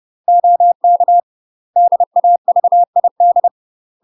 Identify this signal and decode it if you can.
Morse code, heard as a keyed tone, about 30 wpm, 700 Hz. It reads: OK DAVID
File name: UK MASTER QSOs TRAINING / 30WPM / UK MASTER QSOs 0045